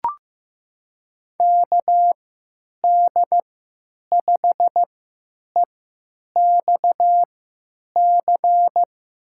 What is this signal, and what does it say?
Morse code, heard as a keyed tone, about 16 wpm, spaced as for 9 wpm, 700 Hz. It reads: KD5EXC
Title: USA Call Signs - 3x Character Spacing - 15wpm